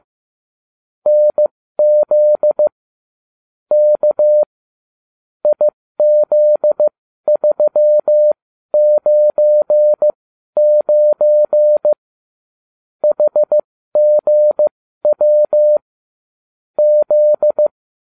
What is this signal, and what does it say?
Morse code, heard as a keyed tone, about 15 wpm, spaced as for 12 wpm, 600 Hz. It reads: NZ K IZ399 HGW Z